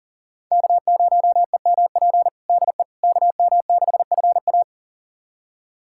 Morse code, heard as a keyed tone, about 40 wpm, 700 Hz. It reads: K0EMP DE KM6FA
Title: USA Callsign DE Callsign - 5 Characters - 40wpm